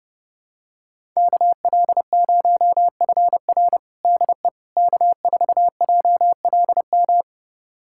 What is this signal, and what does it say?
Morse code, heard as a keyed tone, about 30 wpm, 700 Hz. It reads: KL0FR DE K4JLM
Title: USA Callsign DE Callsign - 5 Characters - 30wpm